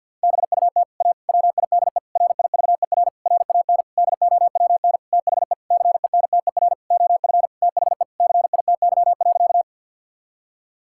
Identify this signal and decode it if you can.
Morse code, heard as a keyed tone, about 50 wpm, 700 Hz. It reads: BUT A WIDE RIVER RAN DOWN THE CENTER OF THE CITY.
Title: Sentences from Top 1000 Words - Review Part 6 - 50wpm